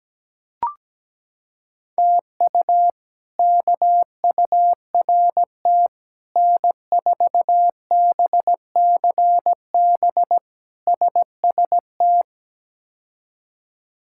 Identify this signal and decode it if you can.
Morse code, heard as a keyed tone, about 17 wpm, 700 Hz. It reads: TU KURT N4BCB SST
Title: K1USN SST QSO Practice - 17wpm